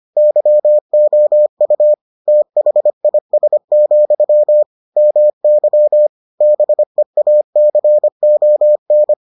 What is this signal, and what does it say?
Morse code, heard as a keyed tone, about 25 wpm, 600 Hz. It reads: YOU THIS, MY BEACON